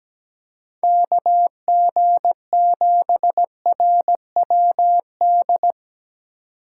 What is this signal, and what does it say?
Morse code, heard as a keyed tone, about 17 wpm, 700 Hz. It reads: KG7RWD
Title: USA Call Signs - 6 Characters - 2x Character Spacing - 17wpm